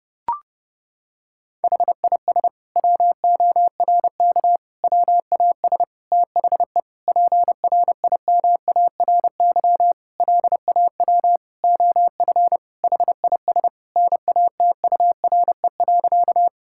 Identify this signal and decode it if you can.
Morse code, heard as a keyed tone, about 30 wpm, 700 Hz. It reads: HIS WORK WAS THE PRIMARY LAW OF HIS NATURE.
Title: Sentences from Top 1000 Words - Review Part 5 - 30wpm